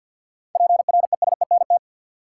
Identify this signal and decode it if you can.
Morse code, heard as a keyed tone, about 48 wpm, 700 Hz. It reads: PRESENT
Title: Seven Letter Words - 50wpm